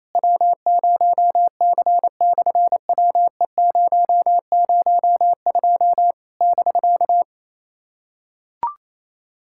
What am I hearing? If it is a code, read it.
Morse code, heard as a keyed tone, about 28 wpm, 700 Hz. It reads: W0C/WE002 <BK>
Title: SOTA QSO Practice 28wpm